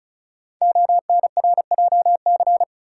Morse code, heard as a keyed tone, about 36 wpm, 700 Hz. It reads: ONRJC
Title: Code Groups - 5 Letters - 35wpm